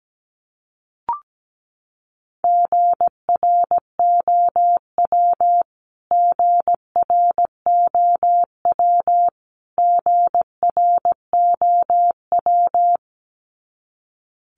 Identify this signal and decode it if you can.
Morse code, heard as a keyed tone, about 17 wpm, 700 Hz. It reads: GROW GROW GROW
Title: Top 500 Words Repeated 17wpm